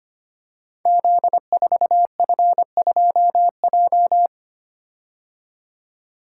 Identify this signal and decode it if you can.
Morse code, heard as a keyed tone, about 25 wpm, 700 Hz. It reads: Z4F2J
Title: Code Groups - 5 Letter:Numbers - 25wpm